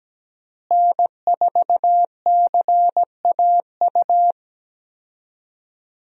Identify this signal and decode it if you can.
Morse code, heard as a keyed tone, about 17 wpm, 700 Hz. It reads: N4CAU
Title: USA Call Signs - 5 Characters - Repeated 2x - 17wpm